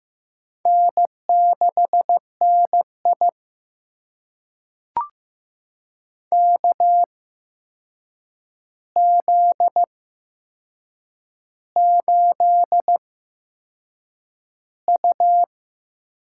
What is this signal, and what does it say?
Morse code, heard as a keyed tone, about 15 wpm, spaced as for 6 wpm, 700 Hz. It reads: N6NI KZ8U
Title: USA Call Signs - 4 Characters - 8x Character Spacing - 15wpm